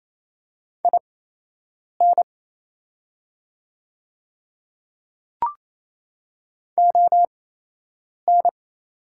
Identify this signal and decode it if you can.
Morse code, heard as a keyed tone, about 27 wpm, spaced as for 8 wpm, 690 Hz. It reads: IN ON
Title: 8x Character Spacing - Top 100 Words - 28wpm